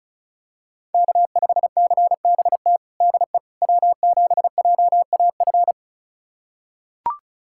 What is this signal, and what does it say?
Morse code, heard as a keyed tone, about 36 wpm, 700 Hz. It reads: K5CBT DE W7JAF
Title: USA Callsign DE Callsign - 5 Characters - 35wpm